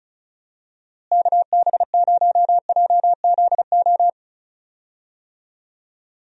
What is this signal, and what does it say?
Morse code, heard as a keyed tone, about 36 wpm, 700 Hz. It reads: KB0JZO
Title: USA Call Signs - 6 Characters - Repeated 2x - 35wpm